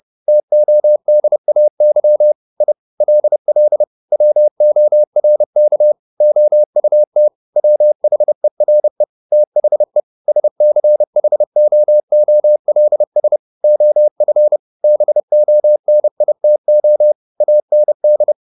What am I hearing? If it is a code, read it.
Morse code, heard as a keyed tone, about 30 wpm, 600 Hz. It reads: TODAY I LL WORK OUT WHERE THE SCHOOLS OF BONITO AND
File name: the old man and the sea / the old man and the sea 1 30wpm